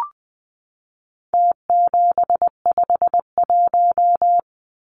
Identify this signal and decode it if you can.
Morse code, heard as a keyed tone, about 20 wpm, 700 Hz. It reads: T751
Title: Numbers - Contest Style - 4-Digit 20wpm